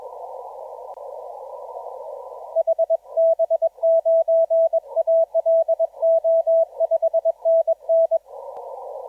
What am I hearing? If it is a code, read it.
Morse code, heard as a keyed tone, about 21 wpm, 660 Hz, with radio noise. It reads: HB9ALO 5NN
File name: TO5PX 1730UTC 17m